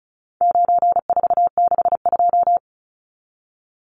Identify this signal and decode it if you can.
Morse code, heard as a keyed tone, about 34 wpm, 700 Hz. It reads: 9462
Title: Numbers - 4-Digit 35wpm